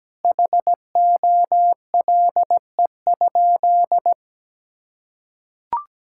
Morse code, heard as a keyed tone, about 17 wpm, 700 Hz. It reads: HOLE?
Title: Sentences from Top 1600 Words 17wpm